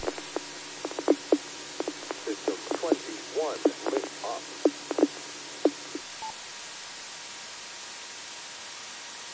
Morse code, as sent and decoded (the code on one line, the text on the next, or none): none